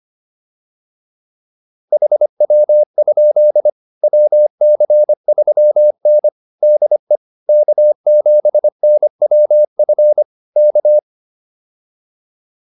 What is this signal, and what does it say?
Morse code, heard as a keyed tone, about 25 wpm, 600 Hz. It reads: HW? WC3N DE K7NWF K